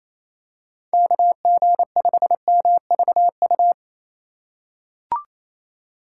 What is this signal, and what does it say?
Morse code, heard as a keyed tone, about 28 wpm, 700 Hz. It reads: KG5MVU
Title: USA Call Signs - 4x Character Spacing - 28wpm